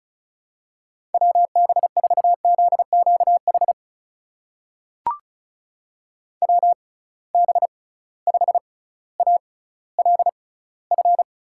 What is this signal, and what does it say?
Morse code, heard as a keyed tone, about 36 wpm, spaced as for 18 wpm, 690 Hz. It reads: WB4ZQH WB5ALF
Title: USA Call Signs - 6 Characters - 6x Character Spacing - 35wpm